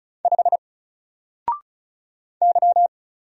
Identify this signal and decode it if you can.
Morse code, heard as a keyed tone, about 36 wpm, 700 Hz. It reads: 5 Y
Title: Single Letter-Number Alt 35wpm